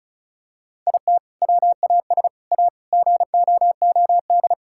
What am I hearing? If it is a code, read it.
Morse code, heard as a keyed tone, about 36 wpm, 700 Hz. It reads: IT WAS A GOOD
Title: Sentences from Top 1000 Words - Review Part 5 - 35wpm